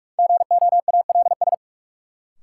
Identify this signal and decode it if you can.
Morse code, heard as a keyed tone, about 41 wpm, 700 Hz. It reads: GOALS
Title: Sentences from Top 1000 Words - Review Part 6 - New Word Drill - 45wpm